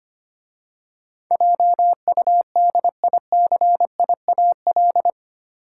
Morse code, heard as a keyed tone, about 25 wpm, 690 Hz. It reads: JUDICIAL
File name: 8 Letter Words - First 3 Letters Given - 25wpm